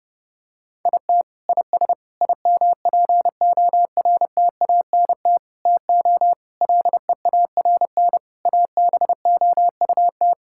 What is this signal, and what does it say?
Morse code, heard as a keyed tone, about 30 wpm, 700 Hz. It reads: IT IS IMPORTANT TO LEARN ABOUT
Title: Sentences from Top 1000 Words - Review Part 6 - 30wpm